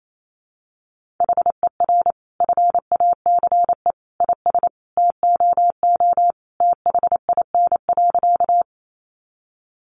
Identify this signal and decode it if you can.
Morse code, heard as a keyed tone, about 28 wpm, 700 Hz. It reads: HER FACE IS TOO THIN.
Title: Sentences from Top 1200 Words 28wpm